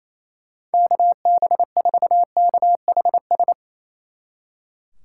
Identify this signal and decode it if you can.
Morse code, heard as a keyed tone, about 37 wpm, 700 Hz. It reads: KB4KHS